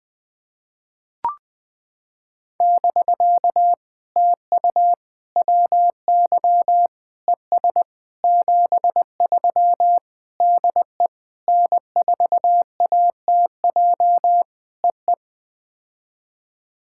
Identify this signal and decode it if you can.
Morse code, heard as a keyed tone, about 20 wpm, 700 Hz. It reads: <BK> TU WY ES 73 DE N4ATJ EE
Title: POTA QSO Practice - 20wpm